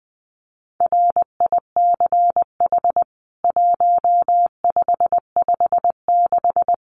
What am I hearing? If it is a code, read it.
Morse code, heard as a keyed tone, about 20 wpm, 700 Hz. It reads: RICH 1556
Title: CWT Exchange 20wpm